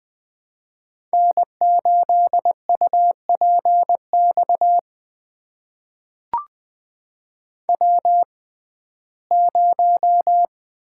Morse code, heard as a keyed tone, about 20 wpm, spaced as for 15 wpm, 700 Hz. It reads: N8UPX W0
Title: USA Call Signs - 5 Characters - 6x Character Spacing - 20wpm